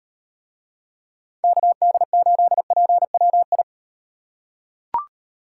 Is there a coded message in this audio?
Yes: Morse code from 1.4 to 3.6 s, heard as a keyed tone, about 35 wpm, 700 Hz. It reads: KD8PWI